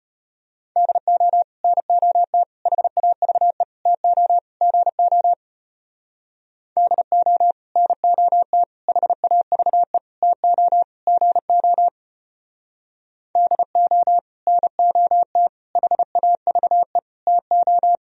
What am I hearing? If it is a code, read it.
Morse code, heard as a keyed tone, about 32 wpm, 700 Hz. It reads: DO NOT HAVE TO GO DO NOT HAVE TO GO DO NOT HAVE TO